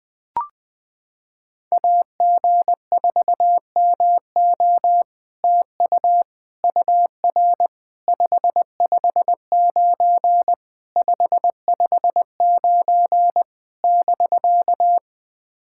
Morse code, heard as a keyed tone, about 20 wpm, 700 Hz. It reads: AG4MO TU UR 559 559 <BK>
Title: SOTA QSO Practice 20wpm